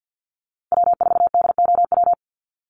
Morse code, heard as a keyed tone, about 50 wpm, 700 Hz. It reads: W4DOW
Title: NA SOTA Chaser Callsigns 50wpm